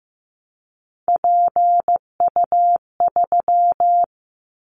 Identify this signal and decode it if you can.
Morse code, heard as a keyed tone, about 15 wpm, 700 Hz. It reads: PU3
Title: Callsign Prefixes for Top 100 DXCC - 15wpm